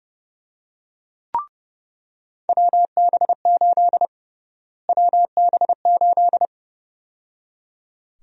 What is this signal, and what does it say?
Morse code, heard as a keyed tone, about 29 wpm, 700 Hz. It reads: WB8 WB8
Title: USA Call Sign Prefixes - Repeated 2x - 30wpm